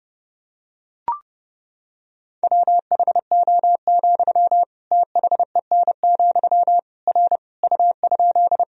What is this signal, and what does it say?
Morse code, heard as a keyed tone, about 30 wpm, 700 Hz. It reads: WHO, THEN, R U?
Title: Sentences from Top 100 Words and Abbreviations 30wpm